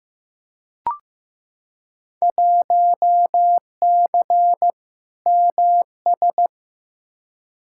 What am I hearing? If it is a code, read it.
Morse code, heard as a keyed tone, about 15 wpm, 700 Hz. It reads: 1C MS
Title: Field Day Exchanges Encore - 15wpm